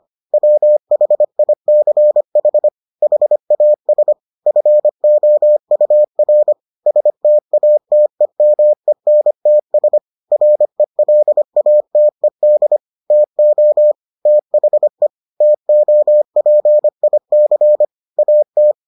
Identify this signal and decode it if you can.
Morse code, heard as a keyed tone, about 25 wpm, 600 Hz. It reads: WHICH HAS FOUR STATEMENTS RELATED TO THE TOPIC AT